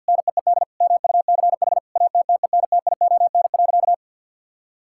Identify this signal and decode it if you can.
Morse code, heard as a keyed tone, about 50 wpm, 700 Hz. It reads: NEED MUCH ATTENTION.
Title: Sentences from Top 1000 Words - Review Part 6 - 50wpm